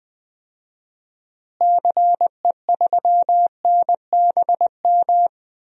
Morse code, heard as a keyed tone, about 20 wpm, 700 Hz. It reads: CE3NBM
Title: DXCC-100 Call Signs - 6 Characters - Repeated 3x - 20wpm